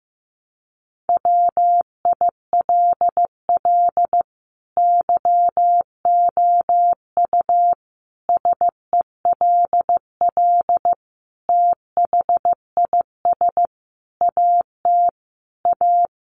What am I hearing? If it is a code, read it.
Morse code, heard as a keyed tone, about 15 wpm, 700 Hz. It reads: WILL YOU SELL THIS AT A
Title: Sentences from Top 1200 Words 15wpm